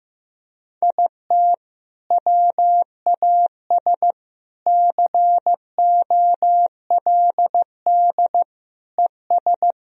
Morse code, heard as a keyed tone, about 15 wpm, 700 Hz. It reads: IT WAS COLD ES
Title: Sentences from Top 500 Words and Abbreviations 15wpm